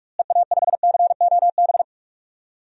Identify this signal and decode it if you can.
Morse code, heard as a keyed tone, about 46 wpm, 700 Hz. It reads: EA5COB
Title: DXCC-100 Call Signs - 6 Characters - Repeated 3x - 45wpm